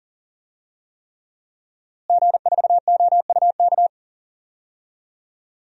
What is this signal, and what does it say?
Morse code, heard as a keyed tone, about 40 wpm, 700 Hz. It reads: G4OUK
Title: DXCC-100 Call Signs - 5 Characters - Repeated 3x - 40wpm